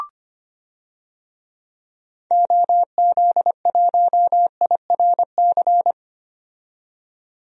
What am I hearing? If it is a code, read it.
Morse code, heard as a keyed tone, about 25 wpm, 700 Hz. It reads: OZ1IRC
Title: DXCC-100 Call Signs - 6 Characters - 25wpm